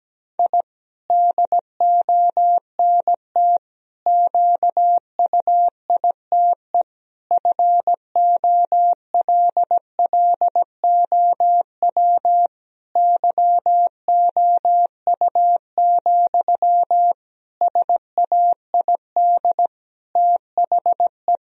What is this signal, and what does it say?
Morse code, heard as a keyed tone, about 17 wpm, 700 Hz. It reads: I DONT QUITE FOLLOW YOU, SAID THE
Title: Sentences from Top 1000 Words - Review Part 4 - 17wpm